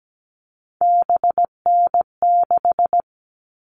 Morse code, heard as a keyed tone, about 17 wpm, 700 Hz. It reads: BN6
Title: All Cut Numbers - 3-Digit 17wpm